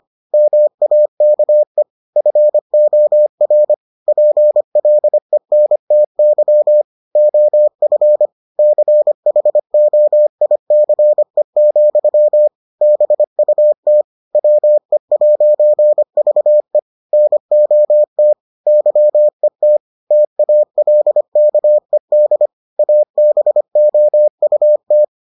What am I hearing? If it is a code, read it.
Morse code, heard as a keyed tone, about 25 wpm, 600 Hz. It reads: MAKE FOR PLENTY OF CHOICE, BUT WE'VE NOT YET TALKED ABOUT